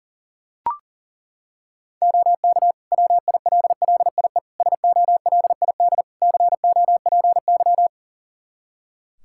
Morse code, heard as a keyed tone, about 37 wpm, 700 Hz. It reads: OK WILLIE SOLID COPY
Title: Practice QSO - Exchange 3 - More Info - 40wpm